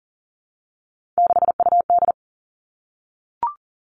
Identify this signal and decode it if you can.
Morse code, heard as a keyed tone, about 38 wpm, 700 Hz. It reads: 6UD
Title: Three Challenging Letter-Number Combinations 40wpm